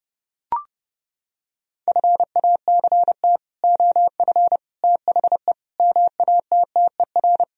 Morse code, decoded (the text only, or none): FACT OF THE MATTER